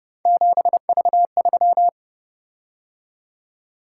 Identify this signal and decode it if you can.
Morse code, heard as a keyed tone, about 30 wpm, 700 Hz. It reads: 7V3